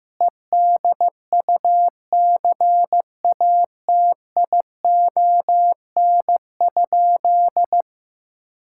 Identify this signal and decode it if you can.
Morse code, heard as a keyed tone, about 15 wpm, 700 Hz. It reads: EDUCATION?
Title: Sentences from Top 1000 Words - Review Part 6 - 15wpm